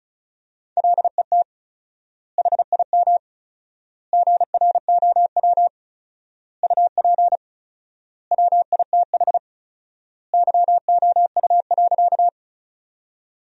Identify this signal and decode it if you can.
Morse code, heard as a keyed tone, about 36 wpm, 700 Hz. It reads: LET HIM GROW UP WITH YOU.
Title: 4x Word-Spacing - Sentences from Top 300 Words - 35wpm